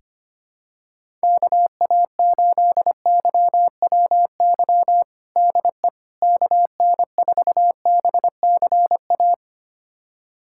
Morse code, heard as a keyed tone, about 25 wpm, 700 Hz. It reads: KA8YWY DE KN4BCA
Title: USA Callsign DE Callsign - 6 Characters - 25wpm